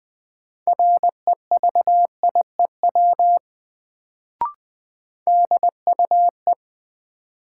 Morse code, heard as a keyed tone, about 20 wpm, 700 Hz. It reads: REVIEW DUE
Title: Sentences from Top 900 Words - New Word Drill - 20wpm